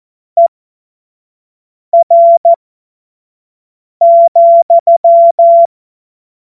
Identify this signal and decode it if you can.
Morse code, heard as a keyed tone, about 14 wpm, spaced as for 5 wpm, 680 Hz. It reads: ER,
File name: Morsekurs / Witze / Witz 28. Dezember 2021, 25 F70